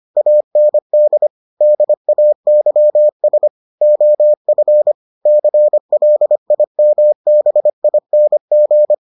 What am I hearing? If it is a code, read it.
Morse code, heard as a keyed tone, about 25 wpm, 600 Hz. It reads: AND DAYS OF CLIMBING